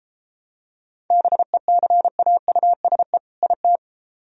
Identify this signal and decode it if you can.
Morse code, heard as a keyed tone, about 29 wpm, 700 Hz. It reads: BECAUSE IT